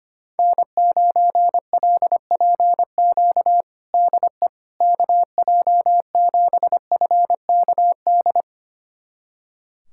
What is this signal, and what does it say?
Morse code, heard as a keyed tone, about 25 wpm, 700 Hz. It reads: N9LPQ DE KJ7FKD
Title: USA Callsign DE Callsign 25wpm